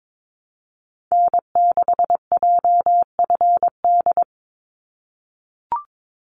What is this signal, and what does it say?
Morse code, heard as a keyed tone, about 22 wpm, 700 Hz. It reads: N6JFD
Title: NA SOTA Activator Callsigns 22wpm